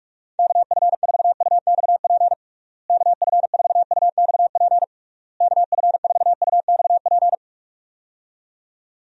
Morse code, heard as a keyed tone, about 41 wpm, 700 Hz. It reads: KF4UXP KF4UXP KF4UXP
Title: USA Call Signs - 6 Characters - Repeated 3x - 45wpm